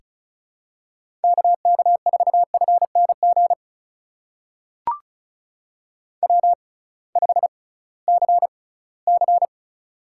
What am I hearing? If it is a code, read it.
Morse code, heard as a keyed tone, about 36 wpm, spaced as for 20 wpm, 700 Hz. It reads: KK4FNG W5CC
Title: USA Call Signs - 6x Character Spacing - 35wpm